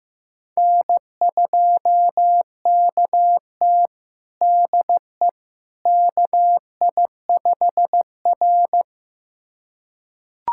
Morse code, heard as a keyed tone, about 15 wpm, 700 Hz. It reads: N2KT DE KI5R
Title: USA Callsign DE Callsign - 4 Characters - 15wpm